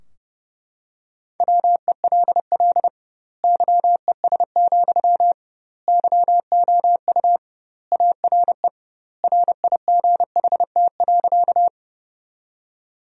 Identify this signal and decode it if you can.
Morse code, heard as a keyed tone, about 30 wpm, 700 Hz. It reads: WELL YES, YOU ARE RIGHT.
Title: 2x Word-Spacing - Sentences from Top 500 Words - 30wpm